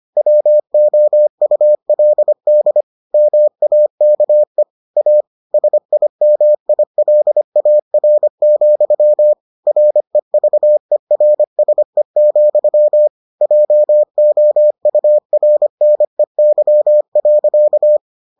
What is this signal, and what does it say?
Morse code, heard as a keyed tone, about 25 wpm, 600 Hz. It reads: WOULD MAKE A SIMILAR, REVERSE, JOURNEY.